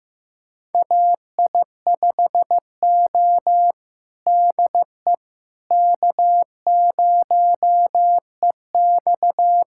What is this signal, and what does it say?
Morse code, heard as a keyed tone, about 15 wpm, 700 Hz. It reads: AI5O DE K0EX
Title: USA Callsign DE Callsign - 4 Characters - 15wpm